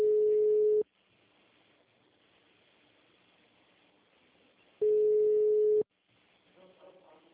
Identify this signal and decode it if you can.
Ringback tone (Europe, 425 Hz), with speech.
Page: Jaguariuna - SP